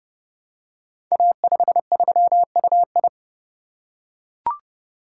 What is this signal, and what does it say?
Morse code, heard as a keyed tone, about 28 wpm, 700 Hz. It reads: A53UI
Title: Code Groups - 5 Letter:Numbers - 30wpm